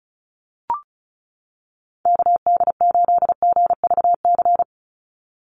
Morse code, heard as a keyed tone, about 34 wpm, 700 Hz. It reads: KD8GVC
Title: USA Call Signs 35wpm